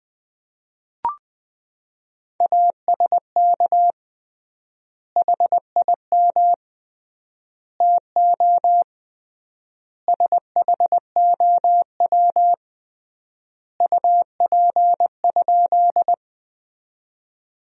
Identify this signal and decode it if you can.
Morse code, heard as a keyed tone, about 20 wpm, 700 Hz. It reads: ASK HIM TO SHOW UP?
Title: 3x Word-Spacing - Sentences from Top 200 Words - 20wpm